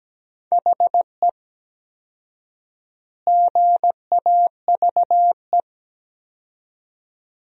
Morse code, heard as a keyed tone, about 17 wpm, 700 Hz. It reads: HE GAVE
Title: 4x Word-Spacing - Sentences from Top 500 Words - 17wpm